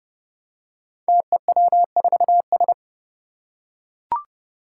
Morse code, heard as a keyed tone, about 30 wpm, 700 Hz. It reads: TEW4S
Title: Five Challenging Letter-Number Combinations 30wpm